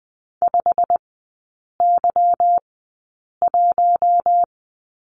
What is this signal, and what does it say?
Morse code, heard as a keyed tone, about 20 wpm, spaced as for 8 wpm, 690 Hz. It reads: 5Y1
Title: 159 Lesson 26e - Letter g - Three Characters 20wpm